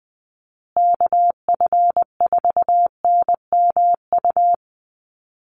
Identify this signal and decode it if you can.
Morse code, heard as a keyed tone, about 20 wpm, 700 Hz. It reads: KF4NMU
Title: USA Call Signs Repeated 3x 20wpm